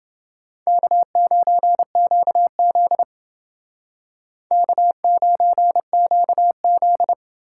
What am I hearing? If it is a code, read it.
Morse code, heard as a keyed tone, about 29 wpm, 700 Hz. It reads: K9QZ K9QZ